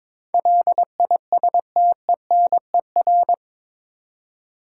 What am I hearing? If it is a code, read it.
Morse code, heard as a keyed tone, about 22 wpm, 700 Hz. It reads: LISTENER
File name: practice / 22wpm / Eight Letter Words - 22wpm